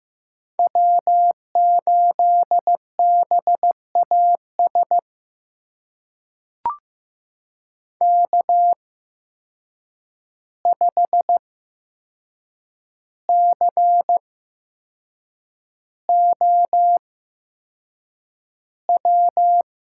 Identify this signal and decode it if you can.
Morse code, heard as a keyed tone, about 15 wpm, spaced as for 6 wpm, 700 Hz. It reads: W8BAS K5COW
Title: USA Call Signs - 5 Characters - 8x Character Spacing - 15wpm